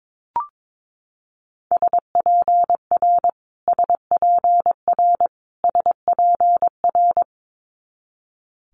Morse code, heard as a keyed tone, about 22 wpm, 700 Hz. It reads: SPR SPR SPR
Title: Multi-letter Phonemes Repeated 3x 22wpm